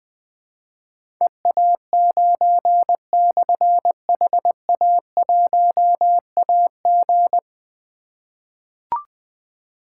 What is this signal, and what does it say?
Morse code, heard as a keyed tone, about 20 wpm, 700 Hz. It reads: EA9/HA1AG
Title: Difficult Call Signs 20wpm